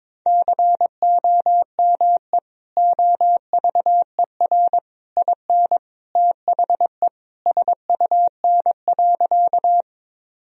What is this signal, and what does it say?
Morse code, heard as a keyed tone, about 22 wpm, 700 Hz. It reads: COME OVER IN THE SUN.
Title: Sentences from Top 1000 Words - Review Part 1 - 22wpm